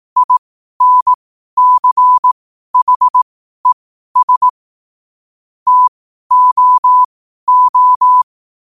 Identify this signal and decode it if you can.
Morse code, heard as a keyed tone, about 18 wpm, spaced as for 12 wpm, 1000 Hz. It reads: INCHES TOO
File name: qotdfiles / 12wpm